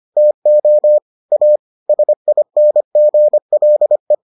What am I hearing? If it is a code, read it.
Morse code, heard as a keyed tone, about 25 wpm, 600 Hz. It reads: TO A SINGLE